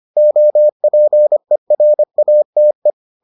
Morse code, heard as a keyed tone, about 25 wpm, 600 Hz. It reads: OPERATE